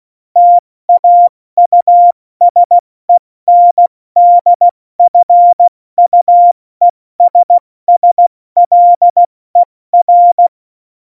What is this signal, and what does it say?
Morse code, heard as a keyed tone, about 16 wpm, spaced as for 14 wpm, 700 Hz. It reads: TAUSENDFUESSLER
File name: Morsekurs / Witze / Witz 27 March 2018, 70 F80